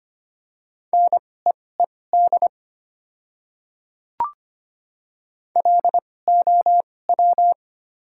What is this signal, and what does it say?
Morse code, heard as a keyed tone, about 25 wpm, spaced as for 18 wpm, 700 Hz. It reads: NEED LOW
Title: 2x Character Spacing - Top 200 Words - 25wpm